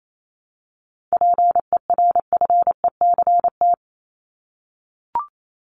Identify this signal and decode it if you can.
Morse code, heard as a keyed tone, about 28 wpm, 700 Hz. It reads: PERFECT
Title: LY Suffix - Top 100 Words with Base Words 28wpm